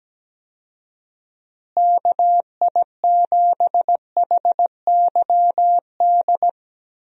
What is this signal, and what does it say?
Morse code, heard as a keyed tone, about 17 wpm, 700 Hz. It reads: KI7HYD
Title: USA Call Signs - 6 Characters - Repeated 3x - 17wpm